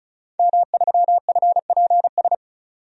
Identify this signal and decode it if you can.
Morse code, heard as a keyed tone, about 36 wpm, 700 Hz. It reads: M3FPS
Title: DXCC-100 Call Signs - 5 Characters - 35wpm